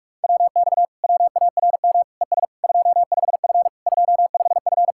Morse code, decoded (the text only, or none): WX WARM ES 25F 25F